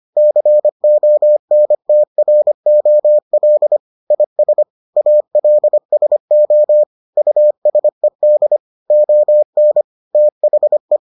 Morse code, heard as a keyed tone, about 25 wpm, 600 Hz. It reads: CONTROL IS ALSO USED ON THE